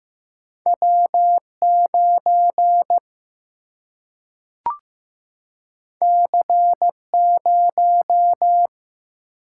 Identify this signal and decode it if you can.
Morse code, heard as a keyed tone, about 15 wpm, 700 Hz. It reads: W 9 C 0
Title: Single-Letter Single-Number 15wpm